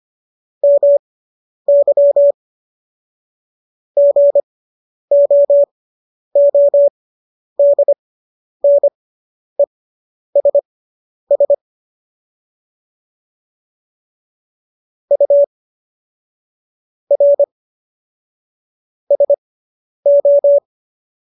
Morse code, heard as a keyed tone, about 25 wpm, spaced as for 10 wpm, 570 Hz. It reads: MY GOODNESS U R SO